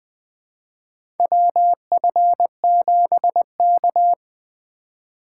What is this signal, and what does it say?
Morse code, heard as a keyed tone, about 21 wpm, 700 Hz. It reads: WF7K
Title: USA Call Signs - 4x Character Spacing - 20wpm